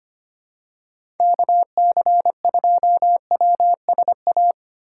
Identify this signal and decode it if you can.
Morse code, heard as a keyed tone, about 25 wpm, 700 Hz. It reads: KC2WSA
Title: USA Call Signs - 6x Character Spacing - 25wpm